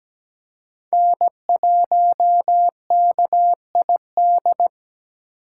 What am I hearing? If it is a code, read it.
Morse code, heard as a keyed tone, about 17 wpm, 700 Hz. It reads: N1KID
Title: USA Call Signs - 5 Characters - 3x Character Spacing - 17wpm